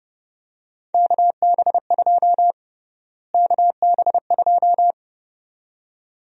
Morse code, heard as a keyed tone, about 30 wpm, 700 Hz. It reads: KB2 KB2
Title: USA Call Sign Prefixes - Repeated 2x - 30wpm